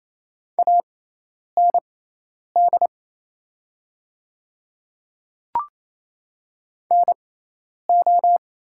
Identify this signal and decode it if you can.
Morse code, heard as a keyed tone, about 28 wpm, spaced as for 10 wpm, 690 Hz. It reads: AND NO
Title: 6x Character Spacing - Top 300 Words - 28wpm